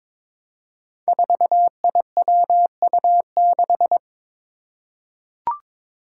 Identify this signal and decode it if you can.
Morse code, heard as a keyed tone, about 22 wpm, 700 Hz. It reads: 4IWU6
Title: Five Challenging Letter-Number Combinations 22wpm